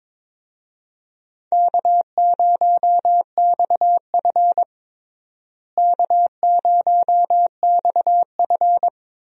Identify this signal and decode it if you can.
Morse code, heard as a keyed tone, about 22 wpm, 700 Hz. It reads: K0XF K0XF
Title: USA Call Signs - 4 Characters - Repeated 2x - 22wpm